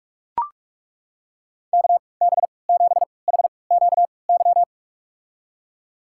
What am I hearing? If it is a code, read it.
Morse code, heard as a keyed tone, about 42 wpm, spaced as for 25 wpm, 700 Hz. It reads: KB7HQY
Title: USA Call Signs - 6 Characters - 3x Character Spacing - 45wpm